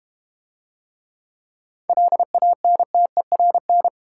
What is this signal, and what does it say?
Morse code, heard as a keyed tone, about 32 wpm, 700 Hz. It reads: LANTERN